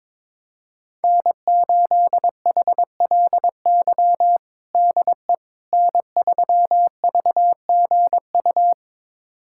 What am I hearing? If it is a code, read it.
Morse code, heard as a keyed tone, about 22 wpm, 700 Hz. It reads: N8HLY DE N3VGU
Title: USA Callsign DE Callsign - 5 Characters - 22wpm